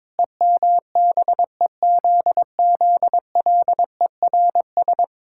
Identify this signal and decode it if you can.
Morse code, heard as a keyed tone, about 22 wpm, 700 Hz. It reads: EMBEZZLERS
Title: Pangram Challenge v2 - 22wpm